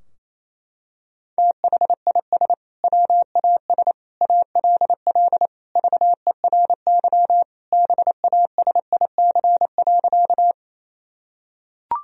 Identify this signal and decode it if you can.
Morse code, heard as a keyed tone, about 28 wpm, 700 Hz. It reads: THIS WAS ALL VERY BASIC.
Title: Sentences from Top 1000 Words - Review Part 3 - 28wpm